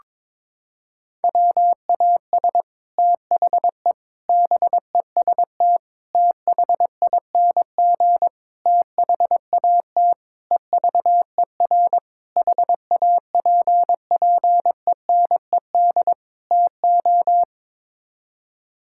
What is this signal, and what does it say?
Morse code, heard as a keyed tone, about 22 wpm, 700 Hz. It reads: WAS THE BEST THING THAT EVER HAPPENED TO
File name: practice / 22wpm / Sets of 8 Words - 22wpm